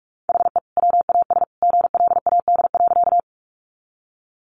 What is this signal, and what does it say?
Morse code, heard as a keyed tone, about 44 wpm, 700 Hz. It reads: HE WAS GLAD.
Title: Sentences from Top 800 Words 45wpm